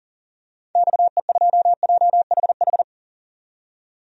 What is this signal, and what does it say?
Morse code, heard as a keyed tone, about 40 wpm, 700 Hz. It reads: XE2JHH
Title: DXCC-100 Call Signs - 6 Characters - 40wpm